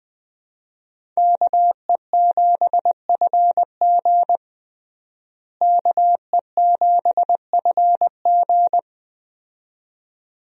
Morse code, heard as a keyed tone, about 20 wpm, 700 Hz. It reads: KE7FG KE7FG
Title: USA Call Signs - 5 Characters - Repeated 2x - 20wpm